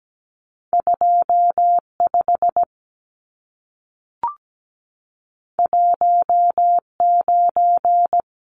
Morse code, heard as a keyed tone, about 17 wpm, 700 Hz. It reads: 25 19
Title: Numbers - 2-Digit 17wpm